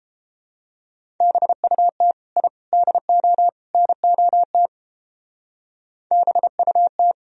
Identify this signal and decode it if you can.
Morse code, heard as a keyed tone, about 31 wpm, 700 Hz. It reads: BUT I DO NOT BUT